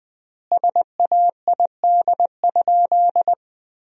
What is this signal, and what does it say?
Morse code, heard as a keyed tone, about 21 wpm, 700 Hz. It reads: SAID?
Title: 4x Word-Spacing - Sentences from Top 100 Words - 20wpm